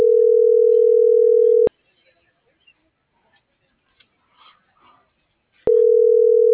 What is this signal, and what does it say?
Ringback tone (North America), 440 + 480 Hz.
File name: CENTRAL BLN / out ericka.canales - .34623